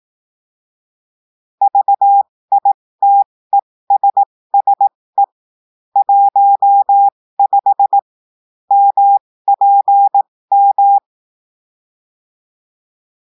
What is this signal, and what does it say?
Morse code, heard as a keyed tone, about 18 wpm, spaced as for 15 wpm, 800 Hz. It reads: VITESSE 15 MPM
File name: cours-du-lundi files / cours-1580-texte-chiffre-15-mpm